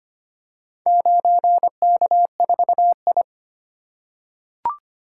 Morse code, heard as a keyed tone, about 25 wpm, 700 Hz. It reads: 9K4I
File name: practice / 25wpm / Four Letter-Number Combinations - 25wpm